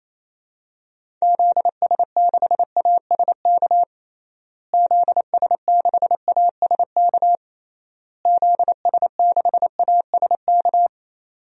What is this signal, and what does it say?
Morse code, heard as a keyed tone, about 27 wpm, 700 Hz. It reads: ZS6ASK ZS6ASK ZS6ASK
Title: DXCC-100 Call Signs - 6 Characters - Repeated 3x - 28wpm